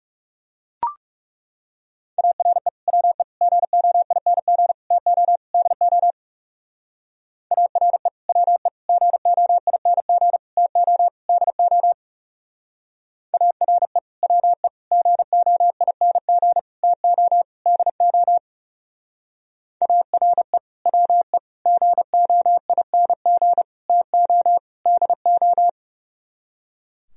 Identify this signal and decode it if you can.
Morse code, heard as a keyed tone, about 36 wpm, 700 Hz. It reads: ARE WE GOING TO DO ARE WE GOING TO DO ARE WE GOING TO DO ARE WE GOING TO DO
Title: Speed-Racing - Sets of 5 Words 30wpm